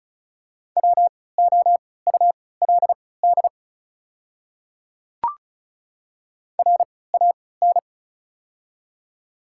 Morse code, heard as a keyed tone, about 36 wpm, spaced as for 20 wpm, 690 Hz. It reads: WOULD RAN
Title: 3x Character Spacing - Top 500 Words - 35wpm